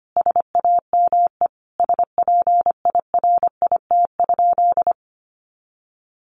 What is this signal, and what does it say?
Morse code, heard as a keyed tone, about 25 wpm, 700 Hz. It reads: SAME SPIRIT?
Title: Sentences from Top 1400 Words 25wpm